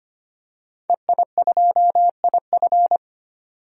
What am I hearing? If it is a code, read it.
Morse code, heard as a keyed tone, about 25 wpm, 700 Hz. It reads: EI2IF
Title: DXCC-100 Call Signs - 5 Characters - 25wpm